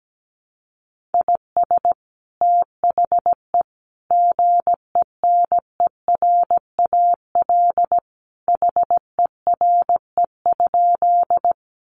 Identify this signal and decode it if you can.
Morse code, heard as a keyed tone, about 17 wpm, 700 Hz. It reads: IS THE GENERAL HERE?
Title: Sentences from Top 600 Words 17wpm